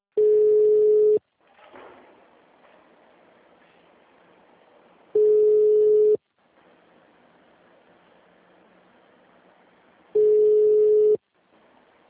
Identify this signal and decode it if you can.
Ringback tone (Europe, 425 Hz), with speech.